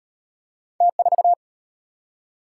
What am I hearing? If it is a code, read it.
Morse code, heard as a keyed tone, about 34 wpm, 700 Hz. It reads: T4